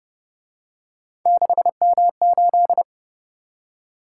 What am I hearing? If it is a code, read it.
Morse code, heard as a keyed tone, about 29 wpm, 700 Hz. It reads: 6M8